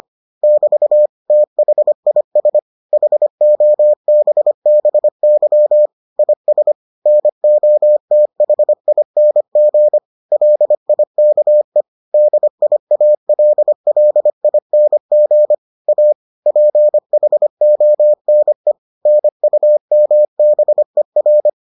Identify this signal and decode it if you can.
Morse code, heard as a keyed tone, about 25 wpm, 600 Hz. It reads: <BT> THIS HOBBY IS NOTHING LIKE DIALLING A PHONE NUMBER